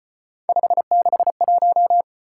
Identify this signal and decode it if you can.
Morse code, heard as a keyed tone, about 33 wpm, 700 Hz. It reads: 561